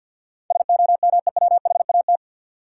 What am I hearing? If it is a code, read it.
Morse code, heard as a keyed tone, about 52 wpm, 700 Hz. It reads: SOMEWHAT
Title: Sentences from Top 1000 Words - Review Part 1 - New Word Drill - 50wpm